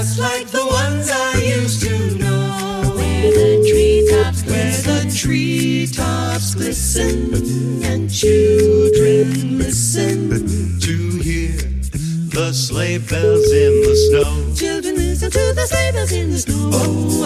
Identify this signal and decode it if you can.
Ringback tone (Europe, 425 Hz), with noise.